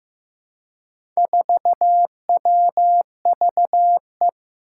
Morse code, heard as a keyed tone, about 15 wpm, 700 Hz. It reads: 4WVE
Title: Four Challenging Letter-Number Combinations 15wpm